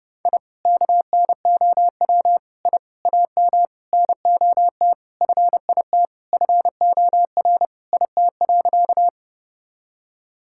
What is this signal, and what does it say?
Morse code, heard as a keyed tone, about 30 wpm, 700 Hz. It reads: I KNOW I AM NOT FIT FOR IT.
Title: Sentences from Top 1000 Words - Review Part 5 - 30wpm